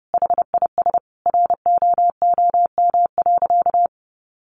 Morse code, heard as a keyed tone, about 30 wpm, 700 Hz. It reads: HIS ROOM.
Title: Sentences from Top 300 Words 30wpm